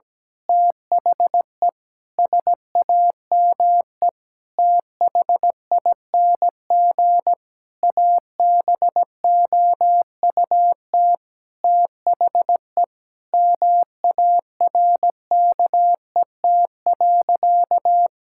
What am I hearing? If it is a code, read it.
Morse code, heard as a keyed tone, about 17 wpm, 700 Hz. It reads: THE SAME THING ABOUT THE MARKET.
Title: Sentences from Top 1000 Words - Review Part 5 - 17wpm